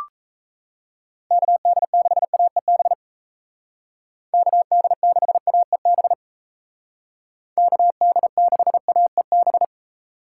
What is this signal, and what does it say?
Morse code, heard as a keyed tone, about 37 wpm, 700 Hz. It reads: KD6AEB KD6AEB KD6AEB